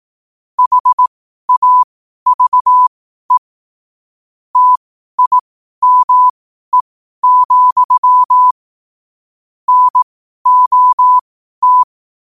Morse code, heard as a keyed tone, about 18 wpm, spaced as for 12 wpm, 1000 Hz. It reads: HAVE TIME, NOT
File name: qotdfiles / 12wpm